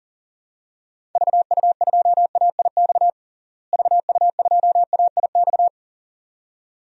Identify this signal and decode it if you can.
Morse code, heard as a keyed tone, about 40 wpm, 700 Hz. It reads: VU2AIX VU2AIX
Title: DXCC-100 Call Signs - 6 Characters - Repeated 2x - 40wpm